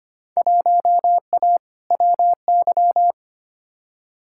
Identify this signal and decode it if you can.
Morse code, heard as a keyed tone, about 25 wpm, 700 Hz. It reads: 1A WY
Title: Field Day Exchanges Encore - 25wpm